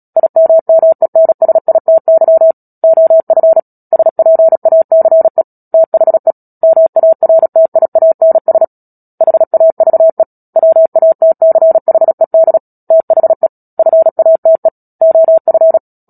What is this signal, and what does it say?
Morse code, heard as a keyed tone, about 37 wpm, 650 Hz. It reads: IMMENSITY OF SPACE THE MARTIANS HAVE WATCHED THE FATE OF
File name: Book Two / 10-Chapter 10 - Epilogue - 36 WPM